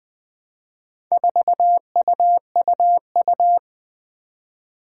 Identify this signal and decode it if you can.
Morse code, heard as a keyed tone, about 20 wpm, 700 Hz. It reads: 4UUU
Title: Four Challenging Letter-Number Combinations 20wpm